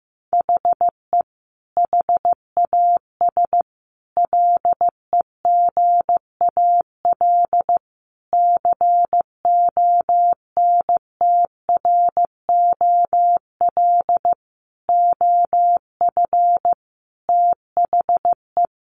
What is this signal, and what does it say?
Morse code, heard as a keyed tone, about 14 wpm, 700 Hz. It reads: HE HAS LEGAL CONTROL OF THE
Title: Sentences from Top 1200 Words 15wpm